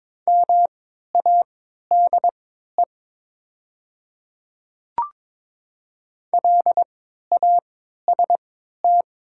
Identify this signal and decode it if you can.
Morse code, heard as a keyed tone, about 22 wpm, spaced as for 12 wpm, 700 Hz. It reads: MADE LAST
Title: 3x Character Spacing - Top 300 Words - 22wpm